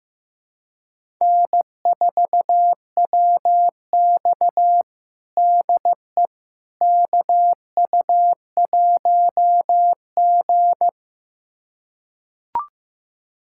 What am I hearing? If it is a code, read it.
Morse code, heard as a keyed tone, about 16 wpm, 700 Hz. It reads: N4WX DE KU1G
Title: USA Callsign DE Callsign - 4 Characters - 15wpm